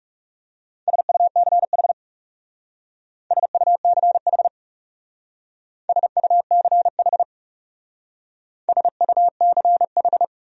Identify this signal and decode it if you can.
Morse code, heard as a keyed tone, about 35 wpm, 700 Hz. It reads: SUCH SUCH SUCH SUCH
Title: Speed-Racing Top 500 Words 30wpm